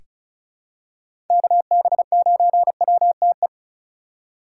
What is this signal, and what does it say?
Morse code, heard as a keyed tone, about 34 wpm, 700 Hz. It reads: KB9WTE
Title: USA Call Signs - 6 Characters - 3x Character Spacing - 35wpm